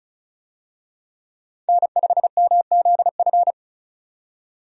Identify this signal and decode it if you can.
Morse code, heard as a keyed tone, about 36 wpm, 700 Hz. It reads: N5MZF
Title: USA Call Signs - 5 Characters - 35wpm